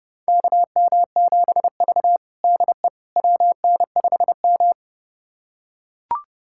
Morse code, heard as a keyed tone, about 30 wpm, 700 Hz. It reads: KM7V DE WN5M
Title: USA Callsign DE Callsign - 4 Characters - 30wpm